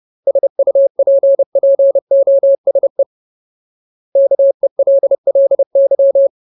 Morse code, heard as a keyed tone, about 30 wpm, 550 Hz. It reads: SUPPOSE KELLY
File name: Ultimate Word List / 30WPM Ultimate